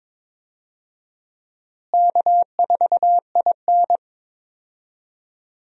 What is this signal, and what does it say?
Morse code, heard as a keyed tone, about 22 wpm, 700 Hz. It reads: K4IN
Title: USA Call Signs - 4 Characters - Repeated 3x - 22wpm